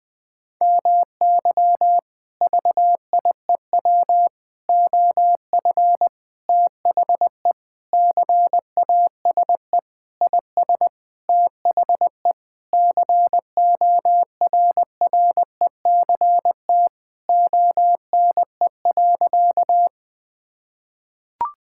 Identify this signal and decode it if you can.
Morse code, heard as a keyed tone, about 20 wpm, 700 Hz. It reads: MY VIEW OF THE CASE IS THE CORRECT ONE.
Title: Sentences from Top 1000 Words - Review Part 6 - 20wpm